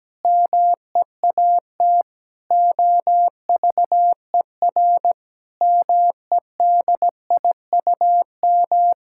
Morse code, heard as a keyed tone, about 17 wpm, 700 Hz. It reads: MEAT OVER MEDIUM
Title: Sentences from Top 1000 Words - Review Part 6 - 17wpm